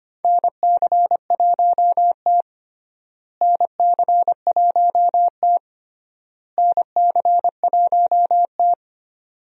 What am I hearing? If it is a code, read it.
Morse code, heard as a keyed tone, about 25 wpm, 700 Hz. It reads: NC1T NC1T NC1T
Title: USA Call Signs - 4 Characters - Repeated 3x - 25wpm